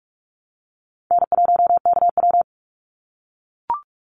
Morse code, heard as a keyed tone, about 44 wpm, 700 Hz. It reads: N1KW
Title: Easy Call Signs - 45wpm